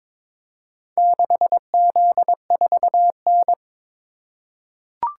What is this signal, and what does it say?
Morse code, heard as a keyed tone, about 22 wpm, 700 Hz. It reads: 6Z4N
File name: practice / 22wpm / Four Letter-Number Combinations - 22wpm